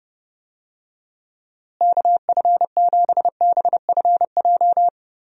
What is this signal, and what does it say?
Morse code, heard as a keyed tone, about 32 wpm, 700 Hz. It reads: KF7BFJ
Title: USA Call Signs - 6 Characters - Repeated 2x - 30wpm